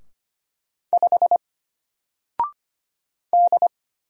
Morse code, heard as a keyed tone, about 25 wpm, 700 Hz. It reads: 5 D
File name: Single-Letter-Number-25wpm